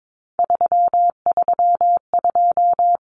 Morse code, heard as a keyed tone, about 22 wpm, 700 Hz. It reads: 332
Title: Common Cut Numbers - 3-Digit 22wpm